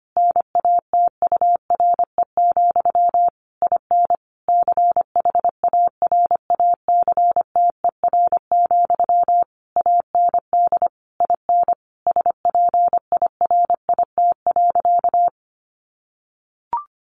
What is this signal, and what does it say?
Morse code, heard as a keyed tone, about 25 wpm, 700 Hz. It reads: NATURE, IN CHARACTER, AND IN SPIRIT.
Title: Sentences from Top 1400 Words 25wpm